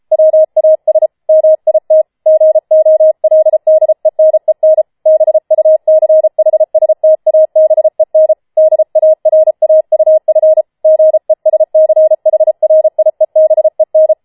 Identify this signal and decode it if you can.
Morse code, heard as a keyed tone, about 33 wpm, 630 Hz. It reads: WAS MIT GOLDENEN BUCHSTABEN DARAUF GESCHRIEBEN
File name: Schneewittchen Teil5 34wpm